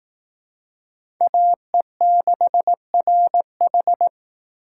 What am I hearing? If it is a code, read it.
Morse code, heard as a keyed tone, about 18 wpm, 700 Hz. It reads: AE6RH
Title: Speed Racing - USA 2x2 Call Signs 15wpm